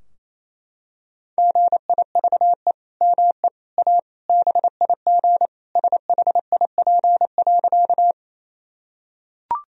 Morse code, heard as a keyed tone, about 28 wpm, 700 Hz. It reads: GIVE ME A BIG SHIP.
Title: Sentences from Top 1000 Words - Review Part 2 - 28wpm